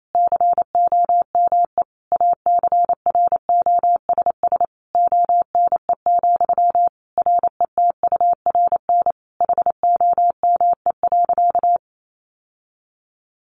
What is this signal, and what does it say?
Morse code, heard as a keyed tone, about 28 wpm, 700 Hz. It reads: COME ACROSS ONE, RETURN HOME.
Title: Sentences from Top 800 Words 28wpm